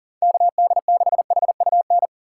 Morse code, heard as a keyed tone, about 40 wpm, 700 Hz. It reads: KD6HUN
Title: USA Call Signs - 6 Characters - 6x Character Spacing - 40wpm